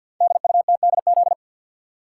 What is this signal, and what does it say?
Morse code, heard as a keyed tone, about 52 wpm, 690 Hz. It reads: DUTDZ
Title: Code Groups - 5 Letters - 50wpm